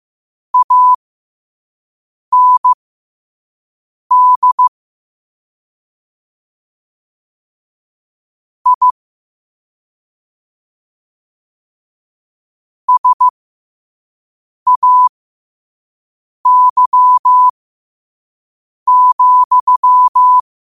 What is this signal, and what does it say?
Morse code, heard as a keyed tone, about 15 wpm, spaced as for 5 wpm, 1000 Hz. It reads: AND I SAY,
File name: qotdfiles / 5wpm